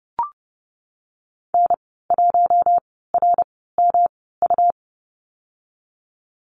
Morse code, heard as a keyed tone, about 31 wpm, spaced as for 17 wpm, 700 Hz. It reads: N1RMU
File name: Lesson 19g - Letter w - Call Signs 30wpm